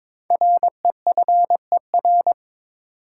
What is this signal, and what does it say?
Morse code, heard as a keyed tone, about 22 wpm, 700 Hz. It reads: REFER
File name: 5 Letter Words - First 3 Letters Given - 22wpm